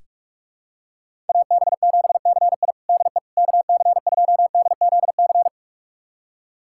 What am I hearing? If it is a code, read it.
Morse code, heard as a keyed tone, about 43 wpm, 700 Hz. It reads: AB7CI DE KC2DZC
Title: USA Callsign DE Callsign 45wpm